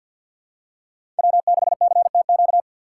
Morse code, heard as a keyed tone, about 47 wpm, 700 Hz. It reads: W6CTQ
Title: USA Call Signs - 5 Characters - 3x Character Spacing - 50wpm